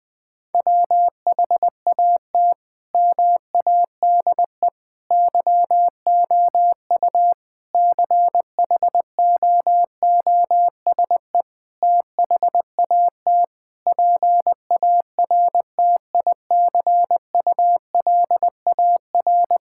Morse code, heard as a keyed tone, about 20 wpm, 700 Hz. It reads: WHAT MADE YOU CHOOSE THAT PARTICULAR
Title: Sentences from Top 1000 Words - Review Part 5 - 20wpm